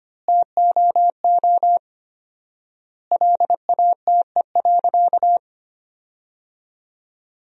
Morse code, heard as a keyed tone, about 25 wpm, 700 Hz. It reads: TOO LATE.
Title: 4x Word-Spacing - Sentences from Top 300 Words - 25wpm